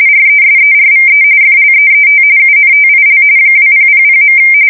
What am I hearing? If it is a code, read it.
RTTY, about 45.45 baud. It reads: RESERVE FLEET UNTIL 2000.
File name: fsk170 / battleshipnj